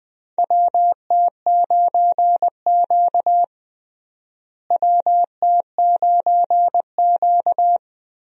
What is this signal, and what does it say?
Morse code, heard as a keyed tone, about 21 wpm, 700 Hz. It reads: WT9Q WT9Q
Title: USA Call Signs - 4 Characters - Repeated 3x - 20wpm